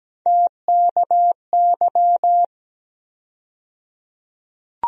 Morse code, heard as a keyed tone, about 17 wpm, 700 Hz. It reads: TKY
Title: USA Call Sign Postfixes - 3 Letters 17wpm